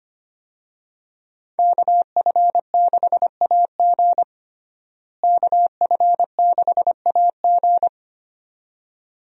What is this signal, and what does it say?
Morse code, heard as a keyed tone, about 25 wpm, 700 Hz. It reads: KF6AG KF6AG
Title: USA Call Signs - 5 Characters - Repeated 2x - 25wpm